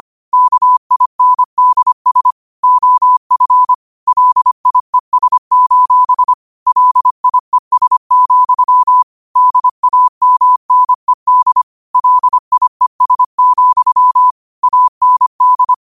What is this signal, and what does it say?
Morse code, heard as a keyed tone, about 25 wpm, 1000 Hz. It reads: KINDS OF LIES: LIES, DAMNED LIES, AND
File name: qotdfiles / 25wpm